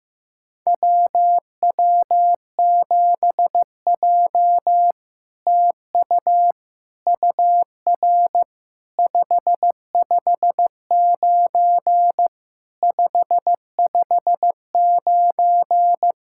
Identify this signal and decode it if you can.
Morse code, heard as a keyed tone, about 15 wpm, 700 Hz. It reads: WW7J TU UR 559 559
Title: SOTA QSO Practice 15wpm